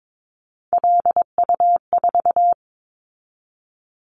Morse code, heard as a keyed tone, about 22 wpm, 700 Hz. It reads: LU4
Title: Callsign Prefixes for Top 50 DXCC - 22wpm